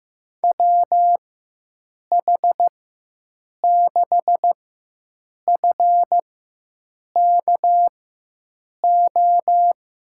Morse code, heard as a keyed tone, about 16 wpm, spaced as for 7 wpm, 700 Hz. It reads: WH6FKO
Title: USA Call Signs - 6 Characters - 4x Character Spacing - 15wpm